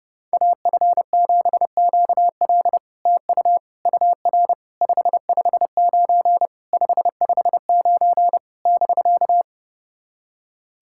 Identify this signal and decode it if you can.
Morse code, heard as a keyed tone, about 32 wpm, 700 Hz. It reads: AF7QL TU UR 559 559 <BK>
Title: SOTA QSO Practice 30wpm